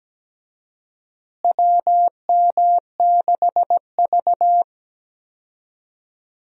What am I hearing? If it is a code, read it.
Morse code, heard as a keyed tone, about 17 wpm, 700 Hz. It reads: WM6V
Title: USA Call Signs - 4 Characters - Repeated 2x - 17wpm